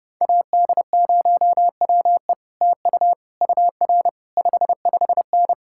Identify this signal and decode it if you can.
Morse code, heard as a keyed tone, about 33 wpm, 700 Hz. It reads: AD0WE TU UR 55N